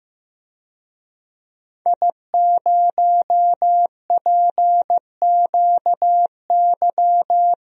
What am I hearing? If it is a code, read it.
Morse code, heard as a keyed tone, about 15 wpm, 700 Hz. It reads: I0PQY
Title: DXCC-100 Call Signs - 5 Characters - 15wpm